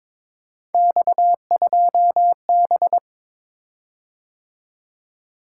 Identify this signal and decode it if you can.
Morse code, heard as a keyed tone, about 22 wpm, 700 Hz. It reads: X2B
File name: practice / 22wpm / Three Letter-Number Combinations - 22wpm